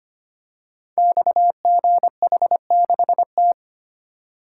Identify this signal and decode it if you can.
Morse code, heard as a keyed tone, about 25 wpm, 700 Hz. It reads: XGH6T
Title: Code Groups - 5 Letter:Numbers - 25wpm